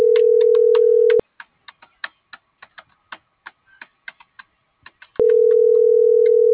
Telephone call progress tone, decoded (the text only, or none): ringback tone